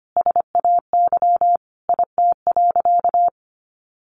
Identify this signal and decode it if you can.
Morse code, heard as a keyed tone, about 25 wpm, 700 Hz. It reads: SAY IT.
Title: Sentences from Top 400 Words 25wpm